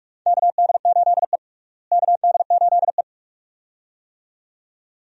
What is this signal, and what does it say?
Morse code, heard as a keyed tone, about 42 wpm, 700 Hz. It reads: KD8E KD8E
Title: USA Call Signs - 4 Characters - Repeated 2x - 45wpm